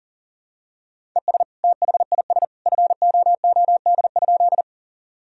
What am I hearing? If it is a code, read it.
Morse code, heard as a keyed tone, about 37 wpm, 700 Hz. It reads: ES THIS FOOD?
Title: Sentences from Top 300 Words and Abbreviations 40wpm